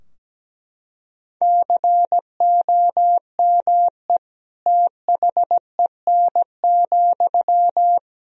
Morse code, heard as a keyed tone, about 17 wpm, 700 Hz. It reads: COME THEN,
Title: Sentences from Top 1000 Words - Review Part 2 - 17wpm